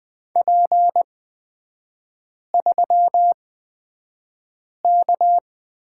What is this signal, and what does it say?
Morse code, heard as a keyed tone, about 20 wpm, spaced as for 5 wpm, 690 Hz. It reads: P3K
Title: 229 Lesson 37e - Letter x - Three Characters 20wpm